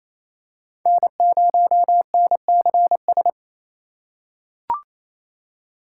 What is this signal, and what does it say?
Morse code, heard as a keyed tone, about 28 wpm, 700 Hz. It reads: N0NCS
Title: USA Call Signs - 5 Characters - 2x Character Spacing - 28wpm